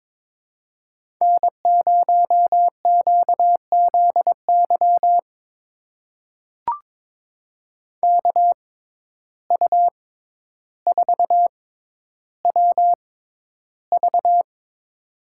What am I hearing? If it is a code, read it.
Morse code, heard as a keyed tone, about 22 wpm, spaced as for 11 wpm, 700 Hz. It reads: N0QZY KU4WV
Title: USA Call Signs - 5 Characters - 6x Character Spacing - 22wpm